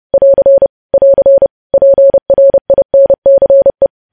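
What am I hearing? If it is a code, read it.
Morse code, heard as a keyed tone, about 30 wpm, 550 Hz. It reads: <AR> <AR> PRINCE
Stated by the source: News headlines sent in Morse code at 30.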